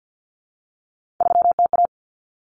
Morse code, heard as a keyed tone, about 50 wpm, 700 Hz. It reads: 3TA
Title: Common Cut Numbers - 3-Digit 50wpm